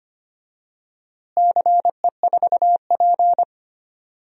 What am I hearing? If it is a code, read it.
Morse code, heard as a keyed tone, about 25 wpm, 700 Hz. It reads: CE4P
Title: DXCC-100 Call Signs - 4 Characters - Repeated 3x - 25wpm